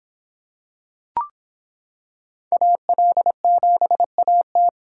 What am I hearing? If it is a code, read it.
Morse code, heard as a keyed tone, about 25 wpm, 700 Hz. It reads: AL7AT